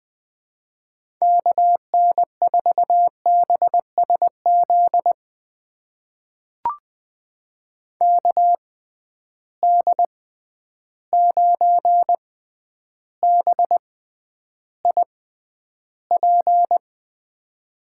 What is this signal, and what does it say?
Morse code, heard as a keyed tone, about 21 wpm, spaced as for 10 wpm, 700 Hz. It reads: KN4BSZ KD9BIP
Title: USA Call Signs - 6 Characters - 6x Character Spacing - 20wpm